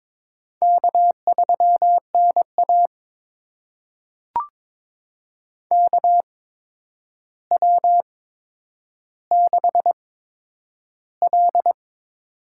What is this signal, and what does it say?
Morse code, heard as a keyed tone, about 22 wpm, spaced as for 9 wpm, 700 Hz. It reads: K3NA KW6L
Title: USA Call Signs - 4 Characters - 8x Character Spacing - 22wpm